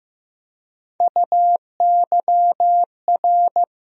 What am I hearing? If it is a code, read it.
Morse code, heard as a keyed tone, about 15 wpm, 700 Hz. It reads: UYR
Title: USA Call Sign Postfixes - 3 Letters 15wpm